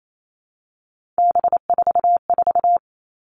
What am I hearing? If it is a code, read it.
Morse code, heard as a keyed tone, about 28 wpm, 700 Hz. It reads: B44
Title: All Cut Numbers - 3-Digit 28wpm